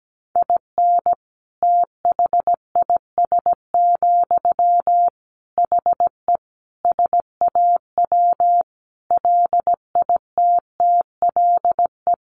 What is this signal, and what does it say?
Morse code, heard as a keyed tone, about 17 wpm, 700 Hz. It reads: IN THIS, HE SAW LITTLE
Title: Sentences from Top 1500 Words 17wpm